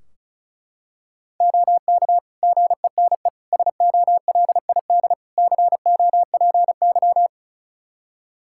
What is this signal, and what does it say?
Morse code, heard as a keyed tone, about 36 wpm, 700 Hz. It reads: OK GENE SOLID COPY
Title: Practice QSO - Exchange 3 - More Info - 35wpm